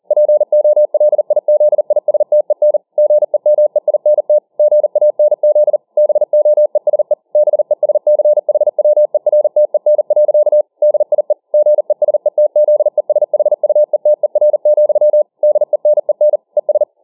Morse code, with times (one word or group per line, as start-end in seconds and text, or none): 0.1-2.8 POLIZISTEN
3.0-4.4 GEMEINT
4.6-5.8 GANZ
6.0-7.1 BOESE
7.4-10.6 BESCHWERTEN.
10.8-11.3 DIE
11.5-15.2 GESETZESHUETER,
15.4-16.4 DENEN
16.6-16.8 ES